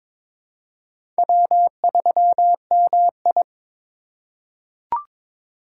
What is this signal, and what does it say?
Morse code, heard as a keyed tone, about 22 wpm, 700 Hz. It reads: W3MI
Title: USA Call Signs - 4 Characters - 3x Character Spacing - 22wpm